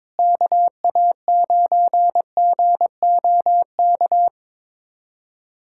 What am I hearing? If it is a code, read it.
Morse code, heard as a keyed tone, about 22 wpm, 700 Hz. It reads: KA9GOK
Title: USA Callsign DE Callsign - 6 Characters - 22wpm